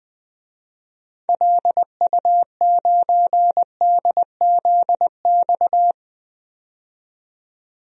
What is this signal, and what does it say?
Morse code, heard as a keyed tone, about 20 wpm, 700 Hz. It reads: LU9DZX
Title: DXCC-100 Call Signs - 6 Characters - 20wpm